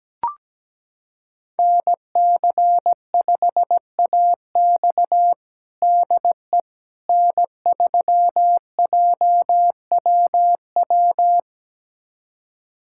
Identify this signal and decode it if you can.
Morse code, heard as a keyed tone, about 17 wpm, 700 Hz. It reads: NC5AX DE N3JWW
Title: USA Callsign DE Callsign - 5 Characters - 17wpm